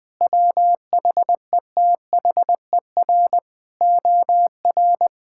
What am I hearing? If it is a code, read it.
Morse code, heard as a keyed tone, about 21 wpm, 700 Hz. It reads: WHETHER OR
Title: Sets of 8 Words - 20wpm